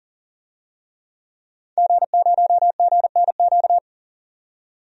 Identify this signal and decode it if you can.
Morse code, heard as a keyed tone, about 40 wpm, 700 Hz. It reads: G0GNQ
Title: DXCC-100 Call Signs - 5 Characters - 40wpm